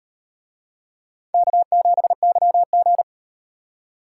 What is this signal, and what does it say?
Morse code, heard as a keyed tone, about 36 wpm, 700 Hz. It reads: K7YG